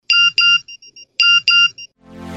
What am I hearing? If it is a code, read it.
There is no code.